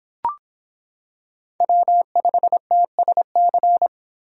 Morse code, heard as a keyed tone, about 26 wpm, 700 Hz. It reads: W5TSC